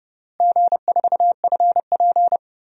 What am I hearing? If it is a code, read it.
Morse code, heard as a keyed tone, about 32 wpm, 700 Hz. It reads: G4FP
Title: DXCC-100 Call Signs - 4 Characters - Repeated 3x - 30wpm